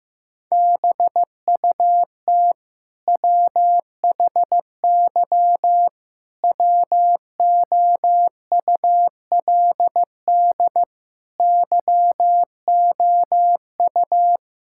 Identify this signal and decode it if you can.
Morse code, heard as a keyed tone, about 15 wpm, 700 Hz. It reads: BUT WHY WOULD YOU
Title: Sentences from Top 1000 Words - Review Part 6 - 15wpm